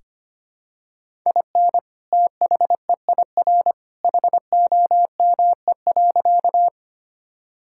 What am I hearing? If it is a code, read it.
Morse code, heard as a keyed tone, about 25 wpm, 700 Hz. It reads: IN THEIR HOME.
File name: practice / 25wpm / Sentences with 2-Word Phrases - 25wpm